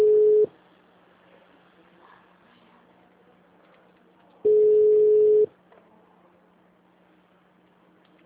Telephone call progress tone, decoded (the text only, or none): ringback tone